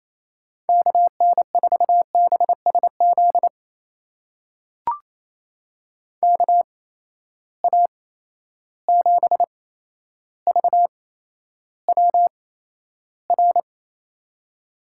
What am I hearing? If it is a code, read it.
Morse code, heard as a keyed tone, about 28 wpm, spaced as for 12 wpm, 700 Hz. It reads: KN4BSZ KA7VWR
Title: USA Call Signs - 6 Characters - 8x Character Spacing - 28wpm